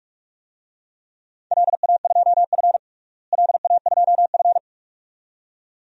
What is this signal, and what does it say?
Morse code, heard as a keyed tone, about 46 wpm, 700 Hz. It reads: LA2F LA2F
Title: DXCC-100 Call Signs - 4 Characters - Repeated 2x - 45wpm